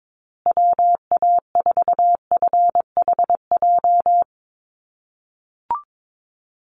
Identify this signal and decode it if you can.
Morse code, heard as a keyed tone, about 22 wpm, 700 Hz. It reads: WA4FHJ
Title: USA Call Signs Repeated 2x 22wpm